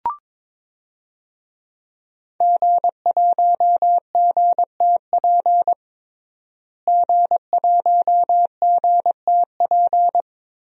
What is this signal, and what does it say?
Morse code, heard as a keyed tone, about 22 wpm, 700 Hz. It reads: G1GTP G1GTP
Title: DXCC-100 Call Signs - 5 Characters - Repeated 3x - 22wpm